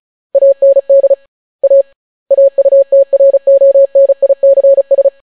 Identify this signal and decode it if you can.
Morse code, heard as a keyed tone, about 36 wpm, 550 Hz. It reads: AND A AUTRONICS